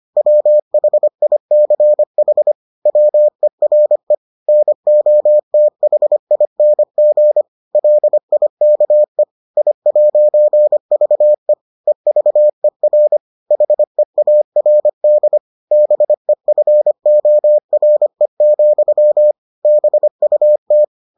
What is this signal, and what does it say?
Morse code, heard as a keyed tone, about 25 wpm, 600 Hz. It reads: WHICH WERE NOTHING LIKE I'VE EVER HEARD BEFORE, BUT